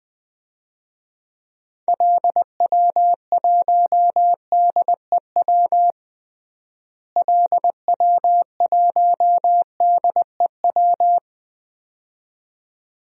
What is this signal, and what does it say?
Morse code, heard as a keyed tone, about 20 wpm, 700 Hz. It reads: LW1DEW LW1DEW
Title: DXCC-100 Call Signs - 6 Characters - Repeated 2x - 20wpm